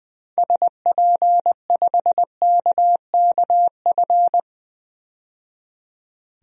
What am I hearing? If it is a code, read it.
Morse code, heard as a keyed tone, about 21 wpm, 700 Hz. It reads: SP5KKF
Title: DXCC-100 Call Signs - 6 Characters - 20wpm